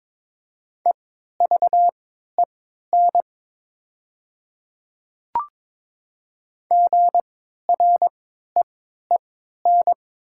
Morse code, heard as a keyed tone, about 22 wpm, spaced as for 12 wpm, 700 Hz. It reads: EVEN GREEN
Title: 3x Character Spacing - Top 500 Words - 22wpm